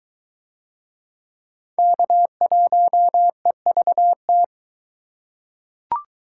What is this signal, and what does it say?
Morse code, heard as a keyed tone, about 22 wpm, 700 Hz. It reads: K1EVT